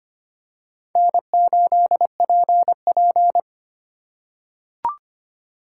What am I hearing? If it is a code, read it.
Morse code, heard as a keyed tone, about 25 wpm, 700 Hz. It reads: N8PP
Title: USA Call Signs - 4 Characters - 2x Character Spacing - 25wpm